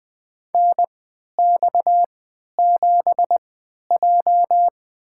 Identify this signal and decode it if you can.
Morse code, heard as a keyed tone, about 20 wpm, spaced as for 11 wpm, 700 Hz. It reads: NX7J
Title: USA Call Signs - 4 Characters - 3x Character Spacing - 20wpm